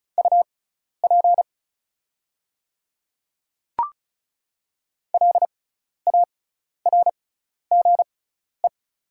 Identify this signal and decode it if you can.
Morse code, heard as a keyed tone, about 34 wpm, spaced as for 12 wpm, 690 Hz. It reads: UP LARGE
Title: 6x Character Spacing - Top 200 Words - 35wpm